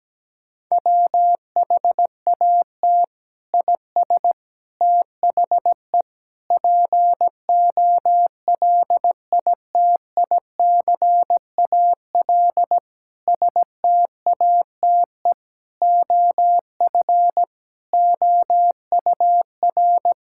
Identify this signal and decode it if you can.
Morse code, heard as a keyed tone, about 17 wpm, 700 Hz. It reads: WHAT IS THE POLITICAL STATE OF OUR
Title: Sentences from Top 1000 Words - Review Part 5 - 17wpm